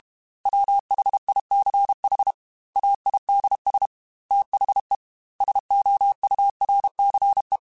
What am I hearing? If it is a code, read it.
Morse code, heard as a keyed tone, about 32 wpm, 800 Hz. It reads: WHICH AIDS THE SOURCE
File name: Technical Subjects / Technical Subjects-mp3s / XC @30